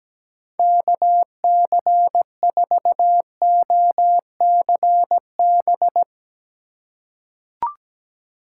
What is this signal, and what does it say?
Morse code, heard as a keyed tone, about 17 wpm, 700 Hz. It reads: KC4OCB
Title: USA Call Signs - 6 Characters - 6x Character Spacing - 17wpm